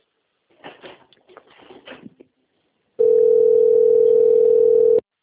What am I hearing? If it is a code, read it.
Ringback tone (North America), 440 + 480 Hz.
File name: out- - - - - .28198